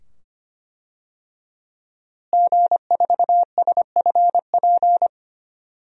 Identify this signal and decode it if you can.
Morse code, heard as a keyed tone, about 25 wpm, 700 Hz. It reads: G4SFP
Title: DXCC-100 Call Signs - 5 Characters - Repeated 2x - 25wpm